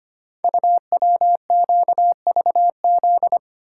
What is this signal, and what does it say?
Morse code, heard as a keyed tone, about 25 wpm, 700 Hz. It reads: UWQVZ
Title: Code Groups - 5 Letters - 25wpm